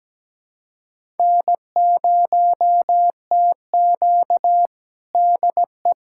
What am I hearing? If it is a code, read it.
Morse code, heard as a keyed tone, about 17 wpm, 700 Hz. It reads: N0TQ DE
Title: USA Callsign DE Callsign - 4 Characters - 17wpm